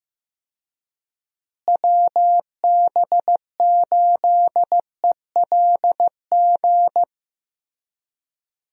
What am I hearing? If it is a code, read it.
Morse code, heard as a keyed tone, about 16 wpm, 700 Hz. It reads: WB8ELG
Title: USA Call Signs - 6 Characters - Repeated 2x - 15wpm